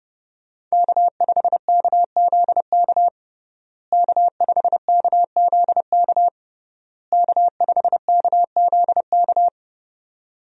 Morse code, heard as a keyed tone, about 30 wpm, 700 Hz. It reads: K5KZK K5KZK K5KZK
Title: USA Call Signs - 5 Characters - Repeated 3x - 30wpm